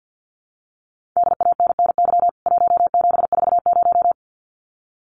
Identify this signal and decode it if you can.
Morse code, heard as a keyed tone, about 49 wpm, 700 Hz. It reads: DANNY 1740
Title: CWT Exchange 50wpm